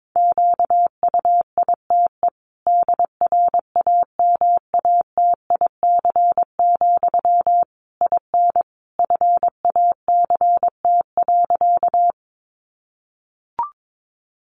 Morse code, heard as a keyed tone, about 22 wpm, 700 Hz. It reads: QUITE DRAMATIC, IN FACT.
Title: Sentences from Top 1300 Words 22wpm